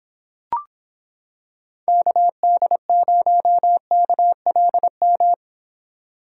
Morse code, heard as a keyed tone, about 26 wpm, 700 Hz. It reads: KD0KLM